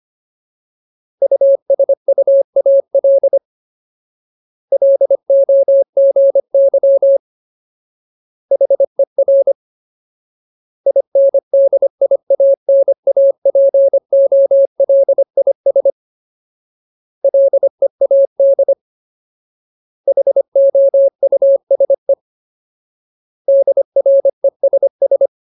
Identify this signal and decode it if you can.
Morse code, heard as a keyed tone, about 25 wpm, 550 Hz. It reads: USUAL LOGY HER INDIANAPOLIS LEAD HOUSE DRESS
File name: Ultimate Word List / 25WPM Ultimate